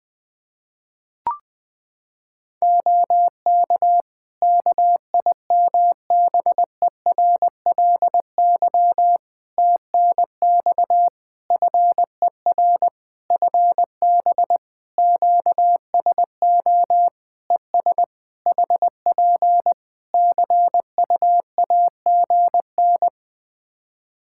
Morse code, heard as a keyed tone, about 20 wpm, 700 Hz. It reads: OK KIMBERLY TNX FER FB QSO ES HP CUAGN
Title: Practice QSO - Exchange 4 - Ending - 20wpm